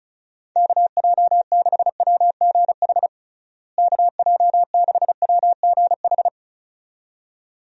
Morse code, heard as a keyed tone, about 36 wpm, 700 Hz. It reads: KJ6WGH KJ6WGH
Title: USA Call Signs - 6 Characters - Repeated 2x - 35wpm